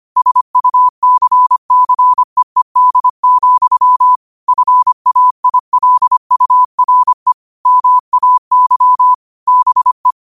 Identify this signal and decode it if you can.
Morse code, heard as a keyed tone, about 25 wpm, 1000 Hz. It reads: SUCCEED, FAILURE MAY BE
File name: qotdfiles / 25wpm